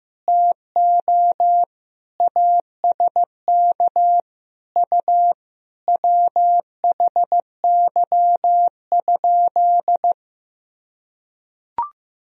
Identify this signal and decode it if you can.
Morse code, heard as a keyed tone, about 15 wpm, 700 Hz. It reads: TO ASK U WHY?
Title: Sentences from Top 200 Words and Abbreviations 15wpm